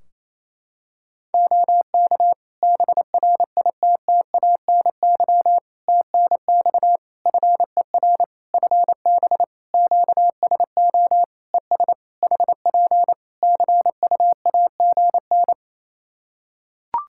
Morse code, heard as a keyed tone, about 28 wpm, 700 Hz. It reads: OK BRITTANY TNX FER FB QSO ES HP CUAGN
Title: Practice QSO - Exchange 4 - Ending - 28wpm